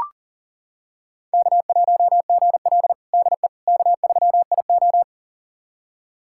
Morse code, heard as a keyed tone, about 40 wpm, 700 Hz. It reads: K1GL DE K3IO
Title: USA Callsign DE Callsign - 4 Characters - 40wpm